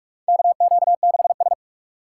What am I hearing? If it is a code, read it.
Morse code, heard as a keyed tone, about 44 wpm, 700 Hz. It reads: KQ6S
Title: USA Call Signs - 4 Characters - 6x Character Spacing - 45wpm